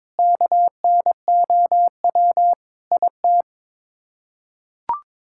Morse code, heard as a keyed tone, about 22 wpm, 700 Hz. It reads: KNOW IT
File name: Sets of 9 Words - 22wpm